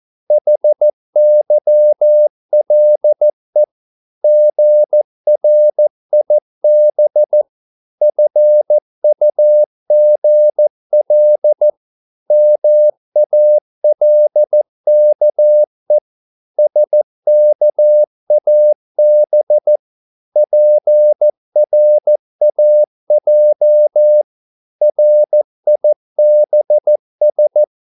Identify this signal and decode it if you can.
Morse code, heard as a keyed tone, about 14 wpm, 600 Hz. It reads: HYLE GRIB FUGL MALKE SKAB PRAJ RIBS
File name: Korte ord DK 14wpm